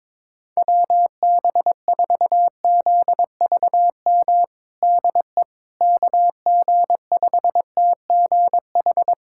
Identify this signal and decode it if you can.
Morse code, heard as a keyed tone, about 22 wpm, 700 Hz. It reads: WB4ZVM DE KG5TGH
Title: USA Callsign DE Callsign - 6 Characters - 22wpm